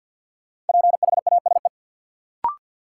Morse code, heard as a keyed tone, about 53 wpm, 700 Hz. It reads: PHASE
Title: Sentences from Top 1200 Words - New Word Drill - 50wpm